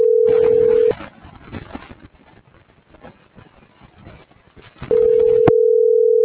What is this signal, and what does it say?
Ringback tone (North America), 440 + 480 Hz.